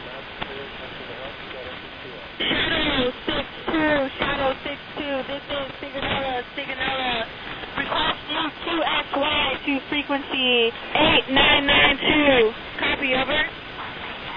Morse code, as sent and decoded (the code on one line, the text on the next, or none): none